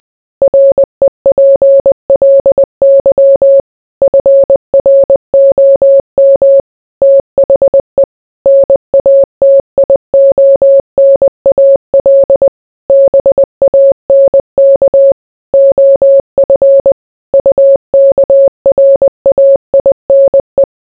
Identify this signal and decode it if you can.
Morse code, heard as a keyed tone, about 20 wpm, 550 Hz. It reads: REPLY FROM THE NATIONAL BANK OF UKRAINE